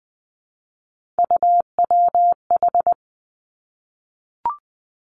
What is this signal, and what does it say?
Morse code, heard as a keyed tone, about 20 wpm, 700 Hz. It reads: UWH
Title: Three Challenging Letter-Number Combinations 20wpm